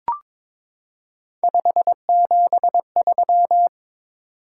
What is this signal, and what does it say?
Morse code, heard as a keyed tone, about 22 wpm, 700 Hz. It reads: 573
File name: practice / 22wpm / Three Letter-Number Combinations - 22wpm